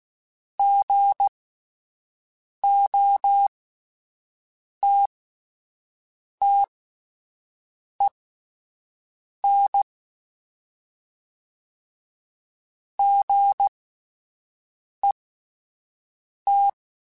Morse code, heard as a keyed tone, about 16 wpm, spaced as for 5 wpm, 800 Hz. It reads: GOTTEN GET
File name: Misc / Misc-mp3s / Gotten @10